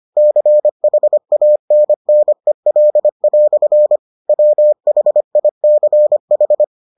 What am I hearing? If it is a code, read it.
Morse code, heard as a keyed tone, about 25 wpm, 600 Hz. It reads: CHANNEL" WHICH